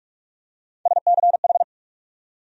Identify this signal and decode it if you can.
Morse code, heard as a keyed tone, about 42 wpm, 700 Hz. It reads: SCH
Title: USA Call Sign Postfixes - 3 Letters 45wpm